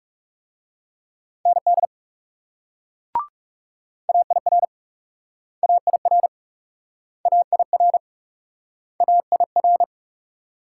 Morse code, heard as a keyed tone, about 41 wpm, 700 Hz. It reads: ND AIR AIR AIR AIR